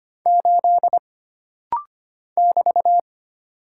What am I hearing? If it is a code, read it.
Morse code, heard as a keyed tone, about 25 wpm, 700 Hz. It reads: 8 <BT>
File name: Single Character (Letter:Number:Punctuation:Prosign) - Rapid-Fire - 25wpm